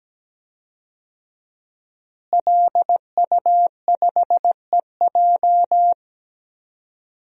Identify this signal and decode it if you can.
Morse code, heard as a keyed tone, about 17 wpm, 700 Hz. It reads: LU5EJ
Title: DXCC-100 Call Signs - 5 Characters - Repeated 2x - 17wpm